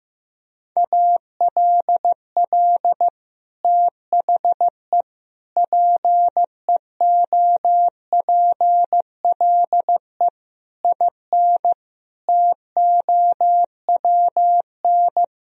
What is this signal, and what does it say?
Morse code, heard as a keyed tone, about 15 wpm, 700 Hz. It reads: ALL THE PEOPLE IN TOWN
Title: Sentences from Top 1000 Words - Review Part 4 - 15wpm